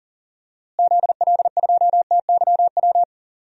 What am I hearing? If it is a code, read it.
Morse code, heard as a keyed tone, about 37 wpm, 700 Hz. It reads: ZL2TYW
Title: DXCC-100 Call Signs - 6 Characters - Repeated 2x - 40wpm